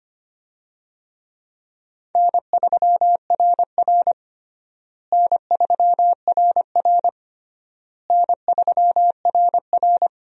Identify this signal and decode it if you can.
Morse code, heard as a keyed tone, about 25 wpm, 700 Hz. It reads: N3RR N3RR N3RR
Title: USA Call Signs - 4 Characters - Repeated 3x - 25wpm